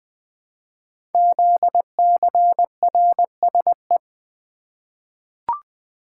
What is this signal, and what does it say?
Morse code, heard as a keyed tone, about 20 wpm, 700 Hz. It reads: ZCRSE
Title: Code Groups - 5 Letters - 20wpm